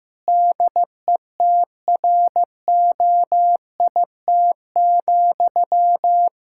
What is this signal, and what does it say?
Morse code, heard as a keyed tone, about 15 wpm, 700 Hz. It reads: DETROIT,
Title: Practice QSO - Exchange 1 - Beginning - 15wpm